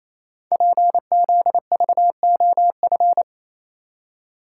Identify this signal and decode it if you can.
Morse code, heard as a keyed tone, about 29 wpm, 700 Hz. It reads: PZVOF
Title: Code Groups - 5 Letters - 28wpm